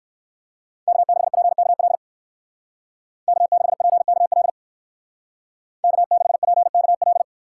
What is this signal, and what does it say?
Morse code, heard as a keyed tone, about 59 wpm, 700 Hz. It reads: K6PKL K6PKL K6PKL